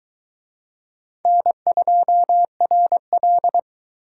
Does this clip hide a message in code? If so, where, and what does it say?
Yes: Morse code from 1.3 to 3.6 s, heard as a keyed tone, about 22 wpm, 700 Hz. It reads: N2RL